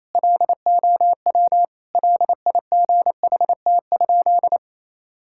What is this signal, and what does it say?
Morse code, heard as a keyed tone, about 27 wpm, 700 Hz. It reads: LOW LIGHT?
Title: Sentences from Top 1000 Words - Review Part 5 - 28wpm